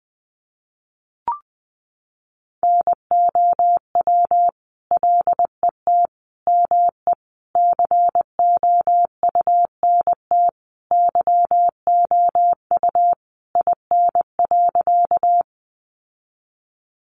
Morse code, heard as a keyed tone, about 19 wpm, 700 Hz. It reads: NOW LET ME COUNT YOU IN.
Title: Sentences from Top 1500 Words 20wpm